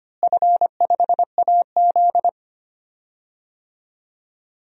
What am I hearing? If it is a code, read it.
Morse code, heard as a keyed tone, about 25 wpm, 700 Hz. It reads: F5AZ
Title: DXCC-100 Call Signs - 4 Characters - Repeated 2x - 25wpm